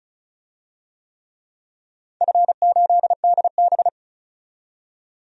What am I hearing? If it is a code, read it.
Morse code, heard as a keyed tone, about 34 wpm, 700 Hz. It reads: F8DB
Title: DXCC-100 Call Signs - 4 Characters - 35wpm